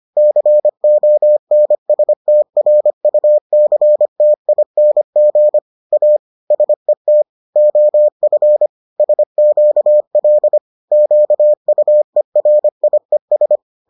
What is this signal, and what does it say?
Morse code, heard as a keyed tone, about 25 wpm, 600 Hz. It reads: CONSTRUCTING A SET OF SQL QUERIES